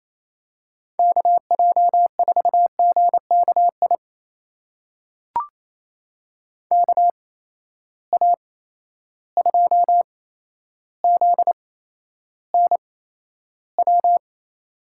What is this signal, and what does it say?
Morse code, heard as a keyed tone, about 27 wpm, spaced as for 12 wpm, 700 Hz. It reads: KJ4GKI KA2ZNW
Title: USA Call Signs - 6 Characters - 8x Character Spacing - 28wpm